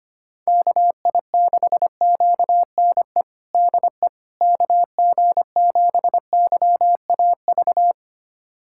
Morse code, heard as a keyed tone, about 25 wpm, 700 Hz. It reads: KI6QNE DE KG7YAV
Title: USA Callsign DE Callsign 25wpm